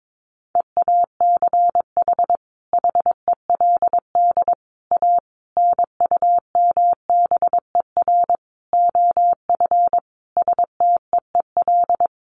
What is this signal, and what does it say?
Morse code, heard as a keyed tone, about 22 wpm, 700 Hz. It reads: EACH HELD A NUMBER OF STEEL
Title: A Sound Of Thunder 22wpm